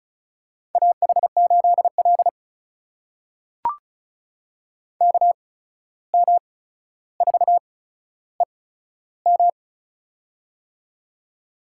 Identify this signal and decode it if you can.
Morse code, heard as a keyed tone, about 36 wpm, spaced as for 14 wpm, 700 Hz. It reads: AH8L KM4EM
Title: USA Call Signs - 8x Character Spacing - 35wpm